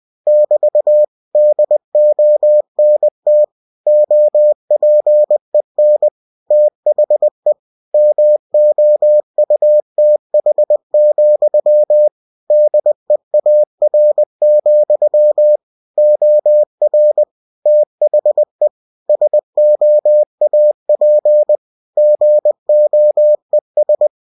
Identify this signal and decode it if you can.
Morse code, heard as a keyed tone, about 20 wpm, 600 Hz. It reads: <BT> DONT OPEN THE MOUTH, DEAR, OR THE SOAP GOES
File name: cw-books / winnie-the-pooh / winnie-20wpm-page-0072